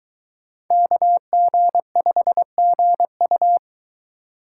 Morse code, heard as a keyed tone, about 24 wpm, 700 Hz. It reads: KG5GU